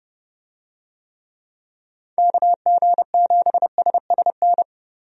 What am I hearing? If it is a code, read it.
Morse code, heard as a keyed tone, about 30 wpm, 700 Hz. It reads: KG7SSN
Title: USA Call Signs - 6 Characters - 30wpm